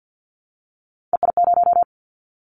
Morse code, heard as a keyed tone, about 50 wpm, 700 Hz. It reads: EI0
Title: Callsign Prefixes for Top 50 DXCC - 50wpm